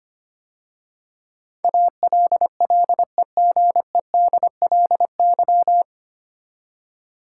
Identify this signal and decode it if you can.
Morse code, heard as a keyed tone, about 25 wpm, 690 Hz. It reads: ALLEGEDLY
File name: practice / 25wpm / 9 Letter Words - First 3 Letters Given - 25wpm